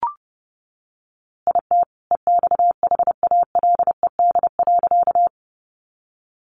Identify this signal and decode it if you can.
Morse code, heard as a keyed tone, about 28 wpm, 700 Hz. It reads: IT EXHALED.
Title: A Sound Of Thunder 30wpm